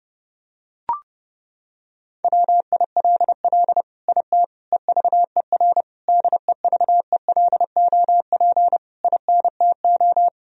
Morse code, heard as a keyed tone, about 30 wpm, 700 Hz. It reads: WILL IT EVER DEVELOP INTO
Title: Sentences from Top 1000 Words - Review Part 2 - 30wpm